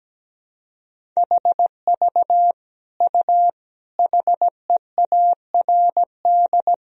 Morse code, heard as a keyed tone, about 17 wpm, 700 Hz. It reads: HV U HEARD
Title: Sentences from Top 400 Words and Abbreviations 17wpm